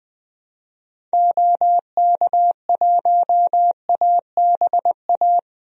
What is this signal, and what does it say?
Morse code, heard as a keyed tone, about 21 wpm, 700 Hz. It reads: OK1ABA
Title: DXCC-100 Call Signs - 6 Characters - Repeated 3x - 20wpm